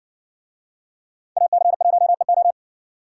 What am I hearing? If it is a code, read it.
Morse code, heard as a keyed tone, about 58 wpm, 700 Hz. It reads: AC1EO